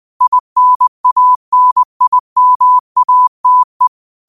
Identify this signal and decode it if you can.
Morse code, heard as a keyed tone, about 20 wpm, 1000 Hz. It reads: INANIMATE
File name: qotdfiles / 20wpm